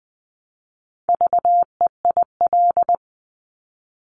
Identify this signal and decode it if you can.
Morse code, heard as a keyed tone, about 21 wpm, 700 Hz. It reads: VEIL
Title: Multi-letter Phonemes Repeated 3x 20wpm